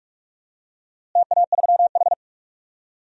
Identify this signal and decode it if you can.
Morse code, heard as a keyed tone, about 43 wpm, 700 Hz. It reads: TA3H
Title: DXCC-100 Call Signs - 4 Characters - Repeated 3x - 45wpm